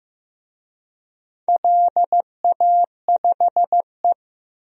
Morse code, heard as a keyed tone, about 15 wpm, 700 Hz. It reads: LA5E
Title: DXCC-100 Call Signs - 4 Characters - Repeated 3x - 15wpm